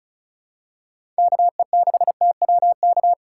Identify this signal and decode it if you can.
Morse code, heard as a keyed tone, about 36 wpm, 700 Hz. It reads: KE6TWK
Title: USA Call Signs - 6 Characters - 8x Character Spacing - 35wpm